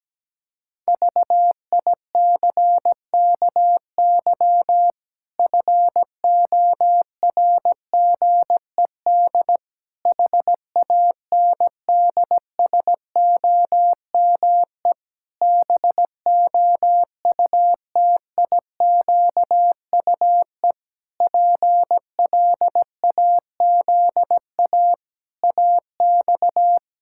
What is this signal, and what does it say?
Morse code, heard as a keyed tone, about 17 wpm, 700 Hz. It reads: VICKY FORGED HANDSOME BOUTIQUE PLAZA AX
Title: Pangram Challenge v2 - 17wpm